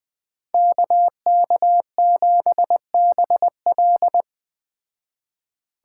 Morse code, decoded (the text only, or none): KK7BL